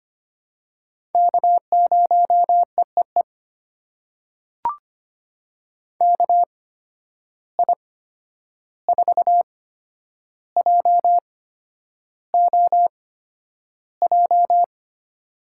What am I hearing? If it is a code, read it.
Morse code, heard as a keyed tone, about 25 wpm, spaced as for 10 wpm, 700 Hz. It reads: K0EEE KI4JOJ
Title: USA Call Signs - 8x Character Spacing - 25wpm